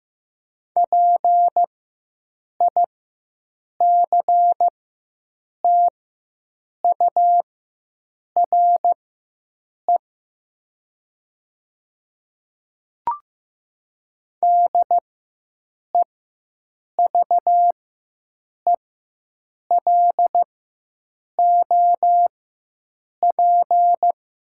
Morse code, heard as a keyed tone, about 15 wpm, spaced as for 7 wpm, 700 Hz. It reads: PICTURE DEVELOP
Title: 4x Character Spacing - Top 500 Words - 15wpm